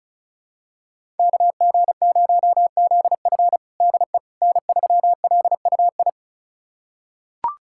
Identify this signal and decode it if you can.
Morse code, heard as a keyed tone, about 36 wpm, 700 Hz. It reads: KG0ZF DE N3LUI
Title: USA Callsign DE Callsign - 5 Characters - 35wpm